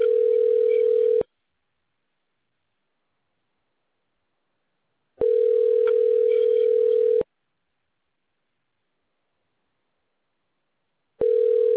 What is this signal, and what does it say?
Ringback tone (North America), 440 + 480 Hz.